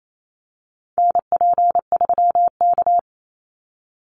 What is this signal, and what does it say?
Morse code, heard as a keyed tone, about 28 wpm, 700 Hz. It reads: NP3K
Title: Call Signs - 28wpm